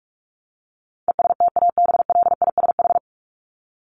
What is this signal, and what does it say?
Morse code, heard as a keyed tone, about 44 wpm, 700 Hz. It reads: ESTABLISH
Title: ED Suffix - Top 100 Words with Base Words 45wpm